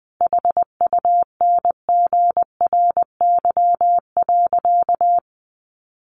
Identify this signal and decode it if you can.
Morse code, heard as a keyed tone, about 20 wpm, 700 Hz. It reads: HUNGRY.
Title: Sentences from Top 1400 Words 20wpm